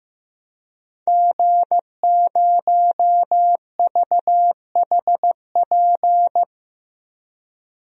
Morse code, heard as a keyed tone, about 15 wpm, 700 Hz. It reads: G0VHP
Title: DXCC-100 Call Signs - 5 Characters - 15wpm